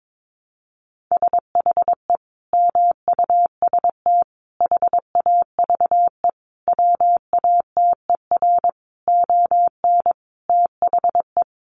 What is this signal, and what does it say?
Morse code, heard as a keyed tone, about 22 wpm, 700 Hz. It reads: SHE MUST HAVE WATER ON THE
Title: Sentences from Top 1100 Words 22wpm